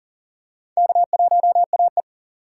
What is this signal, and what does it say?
Morse code, heard as a keyed tone, about 40 wpm, 700 Hz. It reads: K1AE
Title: USA Call Signs - 4 Characters - 2x Character Spacing - 40wpm